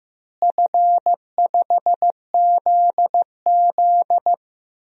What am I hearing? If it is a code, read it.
Morse code, heard as a keyed tone, about 15 wpm, 700 Hz. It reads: F5ZZ
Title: DXCC-100 Call Signs - 4 Characters - Repeated 3x - 15wpm